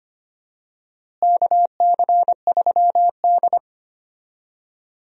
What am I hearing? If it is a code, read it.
Morse code, heard as a keyed tone, about 25 wpm, 700 Hz. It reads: KC3D
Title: USA Call Signs - 4 Characters - Repeated 3x - 25wpm